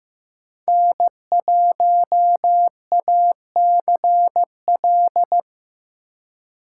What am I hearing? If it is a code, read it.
Morse code, heard as a keyed tone, about 15 wpm, 700 Hz. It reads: N1ACL
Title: USA Call Signs - 5 Characters - 6x Character Spacing - 15wpm